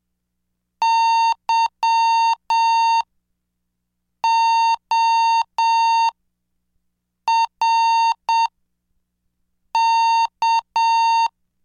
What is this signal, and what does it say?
Morse code, heard as a keyed tone, about 7 wpm, spaced as for 5 wpm, 910 Hz. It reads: YORK
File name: cursos / CDestudo / 20 Track 20